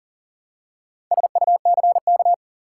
Morse code, heard as a keyed tone, about 40 wpm, 700 Hz. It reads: SUCK
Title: Four Letter Words - 40wpm